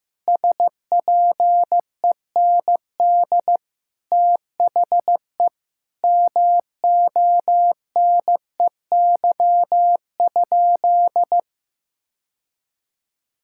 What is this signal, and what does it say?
Morse code, heard as a keyed tone, about 15 wpm, 700 Hz. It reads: SPEND THE MONEY?
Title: Sentences from Top 1000 Words - Review Part 5 - 15wpm